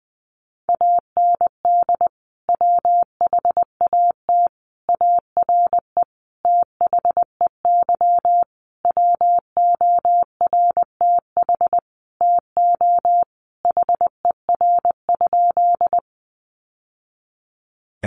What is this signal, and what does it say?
Morse code, heard as a keyed tone, about 20 wpm, 700 Hz. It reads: AND WHAT ARE THEY WORTH TO HER?
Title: Sentences from Top 800 Words 20wpm